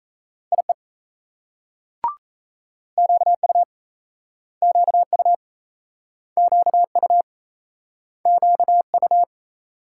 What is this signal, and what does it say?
Morse code, heard as a keyed tone, about 35 wpm, 700 Hz. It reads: IE QU QU QU QU
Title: Speed-Racing Multi-letter Phonemes - 28wpm